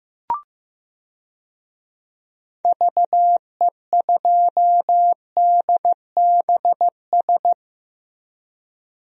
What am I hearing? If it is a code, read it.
Morse code, heard as a keyed tone, about 15 wpm, 700 Hz. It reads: VE2DBS
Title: DXCC-100 Call Signs - 6 Characters - Repeated 2x - 15wpm